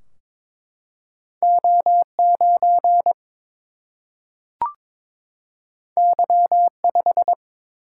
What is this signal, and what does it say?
Morse code, heard as a keyed tone, about 22 wpm, 700 Hz. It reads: O 9 Y 5
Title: Single-Letter Single-Number 22wpm